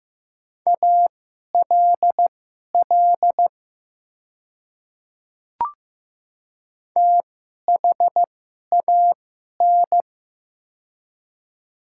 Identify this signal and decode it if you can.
Morse code, heard as a keyed tone, about 15 wpm, spaced as for 11 wpm, 700 Hz. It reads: ALL THAN
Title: 2x Character Spacing - Top 100 Words - 15wpm